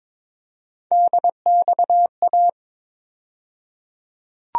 Morse code, heard as a keyed tone, about 22 wpm, 700 Hz. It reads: DXA
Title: USA Call Sign Postfixes - 3 Letters 22wpm